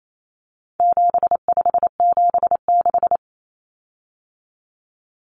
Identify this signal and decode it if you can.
Morse code, heard as a keyed tone, about 28 wpm, 700 Hz. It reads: 7576
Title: Common Cut Numbers - 4-Digit 28wpm